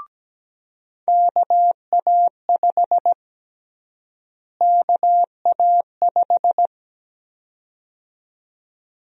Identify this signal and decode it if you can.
Morse code, heard as a keyed tone, about 17 wpm, 700 Hz. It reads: KA5 KA5
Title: USA Call Sign Prefixes - Repeated 2x - 17wpm